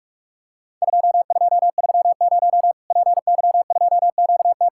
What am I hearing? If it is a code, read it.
Morse code, heard as a keyed tone, about 46 wpm, 700 Hz. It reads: 2230 PY2QT
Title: Worked All Europe DX Contest - QTC Exchange - 45wpm